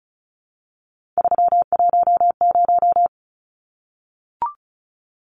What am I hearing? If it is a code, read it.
Morse code, heard as a keyed tone, about 34 wpm, 700 Hz. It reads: 310
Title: Numbers - 3-Digit 35wpm